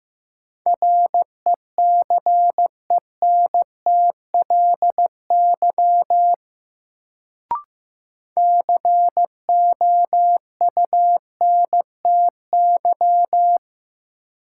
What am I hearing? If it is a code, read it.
Morse code, heard as a keyed tone, about 15 wpm, 700 Hz. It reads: RECENTLY COUNTY
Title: Sentences from Top 1200 Words - New Word Drill - 15wpm